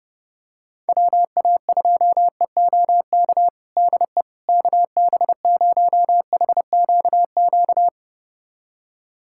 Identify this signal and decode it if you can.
Morse code, heard as a keyed tone, about 32 wpm, 700 Hz. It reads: WA2EOK DE KB0HQQ
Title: USA Callsign DE Callsign - 6 Characters - 30wpm